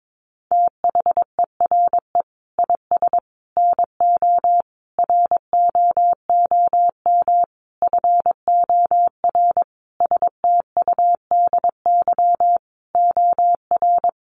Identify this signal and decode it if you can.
Morse code, heard as a keyed tone, about 22 wpm, 700 Hz. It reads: THERE IS NO ROOM FOR STUDY OR
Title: Sentences from Top 1200 Words 22wpm